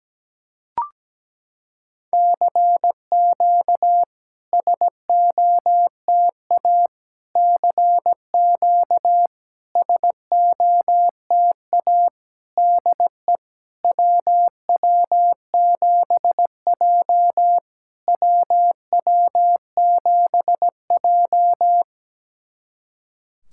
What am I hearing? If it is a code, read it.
Morse code, heard as a keyed tone, about 17 wpm, 700 Hz. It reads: CQ SOTA CQ SOTA DE WW7J WW7J
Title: SOTA QSO Practice 17wpm